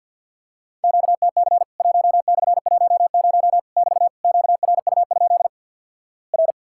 Morse code, heard as a keyed tone, about 50 wpm, 700 Hz. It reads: QTC 1/10 <BT> QRU? R
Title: Worked All Europe DX Contest - QTC Exchange with Repeats - 50wpm